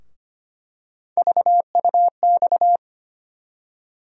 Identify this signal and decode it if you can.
Morse code, heard as a keyed tone, about 25 wpm, 700 Hz. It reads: VUX
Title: USA Call Sign Postfixes - 3 Letters 25wpm